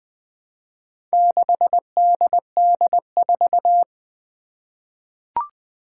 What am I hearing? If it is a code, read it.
Morse code, heard as a keyed tone, about 20 wpm, 700 Hz. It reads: 6DD4
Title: Four Challenging Letter-Number Combinations 20wpm